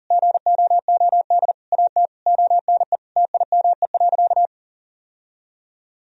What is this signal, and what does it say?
Morse code, heard as a keyed tone, about 40 wpm, 700 Hz. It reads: GOOD AT ONE TIME.
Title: Sentences from Top 1000 Words - Review Part 2 - 40wpm